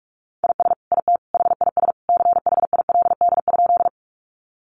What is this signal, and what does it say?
Morse code, heard as a keyed tone, about 44 wpm, 700 Hz. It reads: IS IT HIS CHILD?
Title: Sentences from Top 800 Words 45wpm